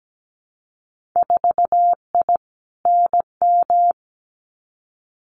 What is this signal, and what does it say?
Morse code, heard as a keyed tone, about 17 wpm, 700 Hz. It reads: 4I NM
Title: Winter Field Day Exchanges Encore 17wpm